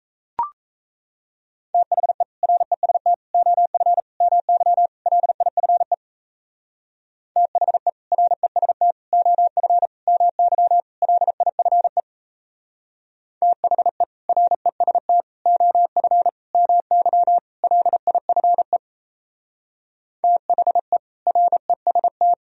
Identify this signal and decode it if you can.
Morse code, heard as a keyed tone, about 34 wpm, 700 Hz. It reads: THE REST OF MY LIFE THE REST OF MY LIFE THE REST OF MY LIFE THE REST
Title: Speed-Racing - Sets of 5 Words 28wpm